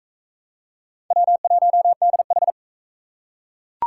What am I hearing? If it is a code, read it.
Morse code, heard as a keyed tone, about 38 wpm, 700 Hz. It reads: W1DH